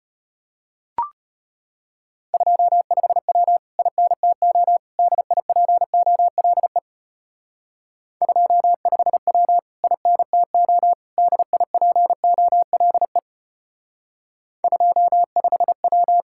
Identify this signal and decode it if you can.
Morse code, heard as a keyed tone, about 31 wpm, 700 Hz. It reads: 25W INTO DIPOLE 25W INTO DIPOLE 25W